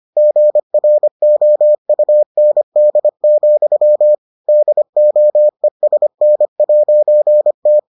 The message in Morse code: GROUND, DOESN'T

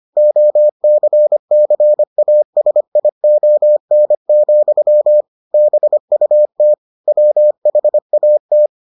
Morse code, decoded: OCCASION, BUT WHAT

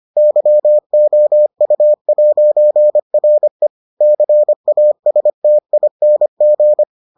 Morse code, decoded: YOU'RE CASTING

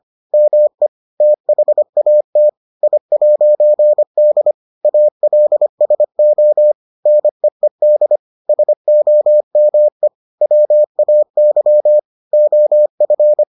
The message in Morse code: ME THAT I'D ALSO NEED SOME WAY OF

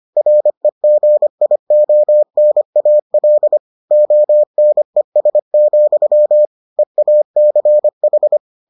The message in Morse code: REGIONAL ONES, EACH

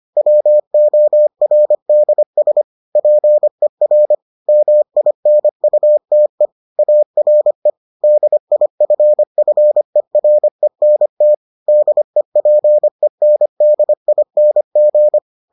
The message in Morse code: WORDS PER MINUTE ARE DIFFERENT DEPENDING

